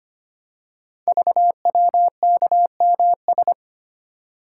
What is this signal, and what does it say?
Morse code, heard as a keyed tone, about 25 wpm, 700 Hz. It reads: VWKMS
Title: Code Groups - 5 Letters - 25wpm